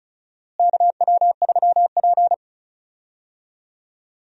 Morse code, heard as a keyed tone, about 36 wpm, 700 Hz. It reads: KW3P
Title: USA Call Signs - 4 Characters - Repeated 2x - 35wpm